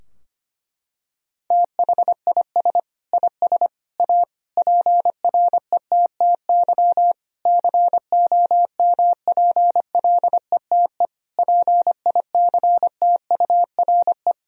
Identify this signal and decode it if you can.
Morse code, heard as a keyed tone, about 25 wpm, 700 Hz. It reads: THIS IS A PRETTY COMPLETE PICTURE
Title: Sentences from Top 1000 Words - Review Part 2 - 25wpm